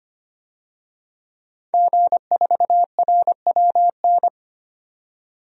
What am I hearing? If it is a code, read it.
Morse code, heard as a keyed tone, about 25 wpm, 700 Hz. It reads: G4RWN
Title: DXCC-100 Call Signs - 5 Characters - 25wpm